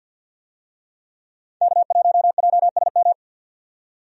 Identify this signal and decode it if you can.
Morse code, heard as a keyed tone, about 51 wpm, 700 Hz. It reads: K1JSM
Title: USA Call Signs - 5 Characters - 50wpm